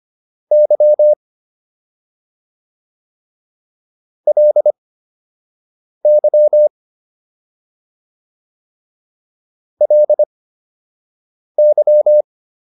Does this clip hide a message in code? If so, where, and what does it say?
Yes: Morse code from 0.5 to 12.2 s, heard as a keyed tone, about 25 wpm, spaced as for 6 wpm, 600 Hz. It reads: Y LY LY